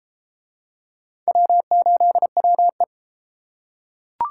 Morse code, heard as a keyed tone, about 31 wpm, 700 Hz. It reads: W8WE